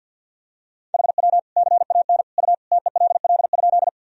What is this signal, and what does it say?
Morse code, heard as a keyed tone, about 51 wpm, 700 Hz. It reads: HW CAN U TELL?
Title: Sentences from Top 200 Words and Abbreviations 50wpm